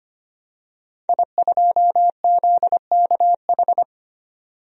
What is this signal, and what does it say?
Morse code, heard as a keyed tone, about 25 wpm, 700 Hz. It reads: I2ZKH
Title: DXCC-100 Call Signs - 5 Characters - 25wpm